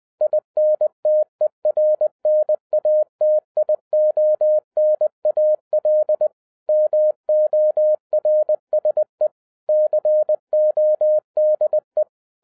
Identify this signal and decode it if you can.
Morse code, heard as a keyed tone, about 20 wpm, 610 Hz. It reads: INTERNATIONAL MORSE CODE